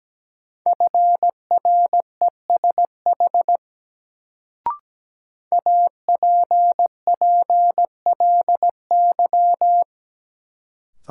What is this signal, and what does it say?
Morse code, heard as a keyed tone, about 17 wpm, 700 Hz. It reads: FRESH APPLY
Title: Sentences from Top 1200 Words - New Word Drill Alt - 17wpm